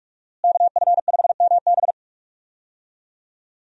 Morse code, heard as a keyed tone, about 42 wpm, 700 Hz. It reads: KF5MB
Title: USA Call Signs - 5 Characters - Repeated 2x - 45wpm